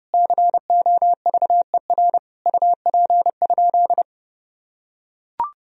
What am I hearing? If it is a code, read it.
Morse code, heard as a keyed tone, about 30 wpm, 700 Hz. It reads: COVER UP?
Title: Sentences from Top 1000 Words - Review Part 1 - 30wpm